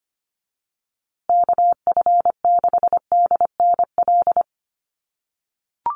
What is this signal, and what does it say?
Morse code, heard as a keyed tone, about 25 wpm, 700 Hz. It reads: KF6DNL
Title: USA Call Signs Repeated 3x 25wpm